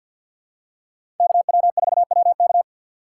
Morse code, heard as a keyed tone, about 52 wpm, 700 Hz. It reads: KW4WK
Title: USA Call Signs - 5 Characters - 4x Character Spacing - 50wpm